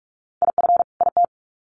Morse code, heard as a keyed tone, about 44 wpm, 700 Hz. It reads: IF IT
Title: Sets of 2 Words 45wpm